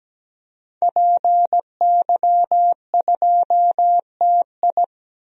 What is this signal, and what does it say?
Morse code, heard as a keyed tone, about 17 wpm, 700 Hz. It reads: PY2TI
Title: DXCC-100 Call Signs - 5 Characters - 17wpm